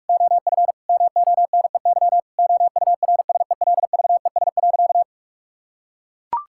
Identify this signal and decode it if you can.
Morse code, heard as a keyed tone, about 41 wpm, 700 Hz. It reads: OF MONEY OURSELVES.
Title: Sentences from Top 1600 Words 45wpm